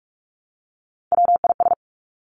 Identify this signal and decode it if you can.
Morse code, heard as a keyed tone, about 46 wpm, 700 Hz. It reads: WIS
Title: Three Challenging Letter-Number Combinations 45wpm